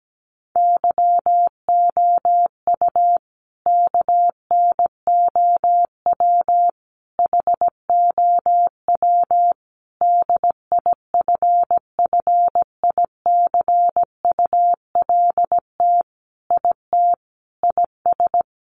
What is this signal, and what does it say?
Morse code, heard as a keyed tone, about 17 wpm, 700 Hz. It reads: YOU KNOW HOW DIFFICULT IT IS
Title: Sentences from Top 1100 Words 17wpm